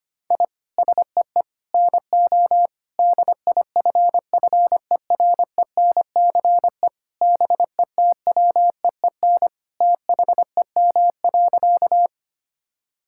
Morse code, heard as a keyed tone, about 25 wpm, 700 Hz. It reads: I SEE NO DIFFERENCE BETWEEN THEM.
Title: Sentences from Top 1000 Words - Review Part 5 - 25wpm